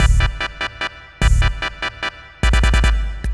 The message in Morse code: none